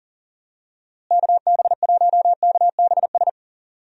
Morse code, heard as a keyed tone, about 40 wpm, 700 Hz. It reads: KB1KBS
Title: USA Call Signs - 6 Characters - Repeated 3x - 40wpm